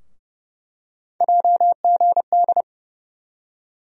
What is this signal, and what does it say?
Morse code, heard as a keyed tone, about 32 wpm, 700 Hz. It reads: JGD
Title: USA Call Sign Postfixes - 3 Letters 30wpm